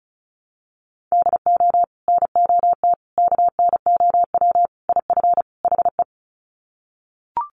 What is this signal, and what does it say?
Morse code, heard as a keyed tone, about 34 wpm, 700 Hz. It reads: DO NOT KNOW IF HE
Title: Sets of 5 Words Encore2 35wpm